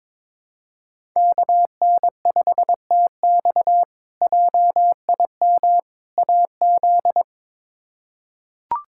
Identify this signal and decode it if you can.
Morse code, heard as a keyed tone, about 22 wpm, 700 Hz. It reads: KN5TX JIM AZ
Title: K1USN SST QSO Practice - 22wpm